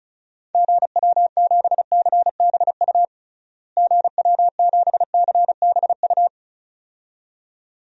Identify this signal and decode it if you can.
Morse code, heard as a keyed tone, about 34 wpm, 700 Hz. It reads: GW7CBU GW7CBU
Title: DXCC-100 Call Signs - 6 Characters - Repeated 2x - 35wpm